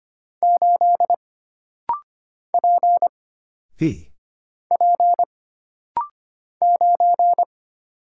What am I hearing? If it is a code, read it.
Morse code, heard as a keyed tone, about 25 wpm, 700 Hz. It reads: 8 P P 9
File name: Single Character (Letter:Number:Punctuation:Prosign) - Rapid-Fire - 25wpm